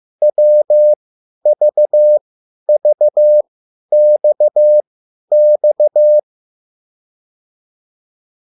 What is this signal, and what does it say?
Morse code, heard as a keyed tone, about 15 wpm, spaced as for 10 wpm, 600 Hz. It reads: WVVXX